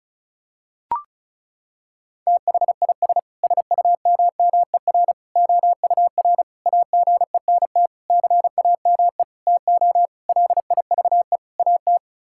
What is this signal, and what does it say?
Morse code, heard as a keyed tone, about 34 wpm, 700 Hz. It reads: THIS SUMMER OUR AGENT CAME TO LIVE AT
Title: Sentences from Top 1600 Words 35wpm